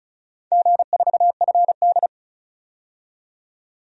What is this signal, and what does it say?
Morse code, heard as a keyed tone, about 36 wpm, 700 Hz. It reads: G4FD
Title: DXCC-100 Call Signs - 4 Characters - 35wpm